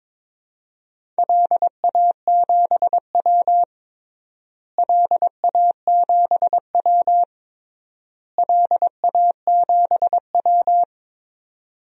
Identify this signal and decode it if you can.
Morse code, heard as a keyed tone, about 23 wpm, 700 Hz. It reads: LA7W LA7W LA7W
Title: DXCC-100 Call Signs - 4 Characters - Repeated 3x - 22wpm